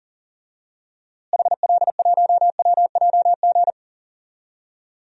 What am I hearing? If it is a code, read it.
Morse code, heard as a keyed tone, about 40 wpm, 700 Hz. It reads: HL1WJG
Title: DXCC-100 Call Signs - 6 Characters - Repeated 2x - 40wpm